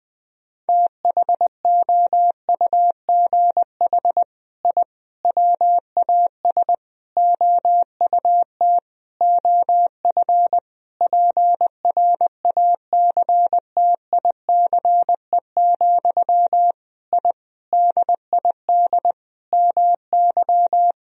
Morse code, heard as a keyed tone, about 21 wpm, 700 Hz. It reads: THOUGH I WAS OUT OF PRACTICE, I DID MY
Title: Sentences from Top 1000 Words - Review Part 4 - 20wpm